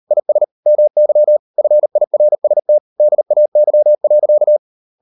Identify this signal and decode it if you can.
Morse code, heard as a keyed tone, about 38 wpm, 600 Hz. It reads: IS MY FIRST DAY.